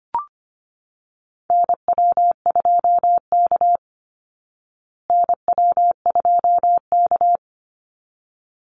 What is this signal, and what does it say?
Morse code, heard as a keyed tone, about 25 wpm, 700 Hz. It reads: NW2K NW2K
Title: Call Signs Repeated 3x - 25wpm